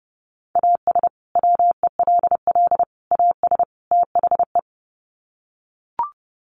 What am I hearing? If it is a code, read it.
Morse code, heard as a keyed tone, about 30 wpm, 700 Hz. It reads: AS WELL AS THE
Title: Sets of 4 Words 30wpm